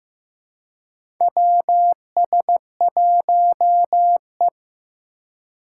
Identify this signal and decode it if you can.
Morse code, heard as a keyed tone, about 15 wpm, 700 Hz. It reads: WS1E
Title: USA Call Signs - 4 Characters - 2x Character Spacing - 15wpm